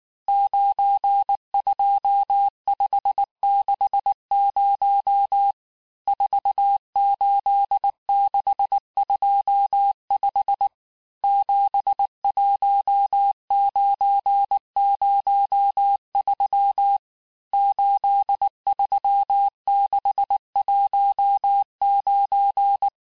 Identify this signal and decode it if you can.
Morse code, heard as a keyed tone, about 19 wpm, 800 Hz. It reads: 92560 48625 71903 83619
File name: cours-du-lundi files / cours-1589-texte-chiffre-vitesse-18mpm